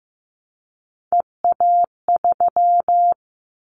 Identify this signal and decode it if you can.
Morse code, heard as a keyed tone, about 15 wpm, 700 Hz. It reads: EA3
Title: Callsign Prefixes for Top 25 DXCC - 15wpm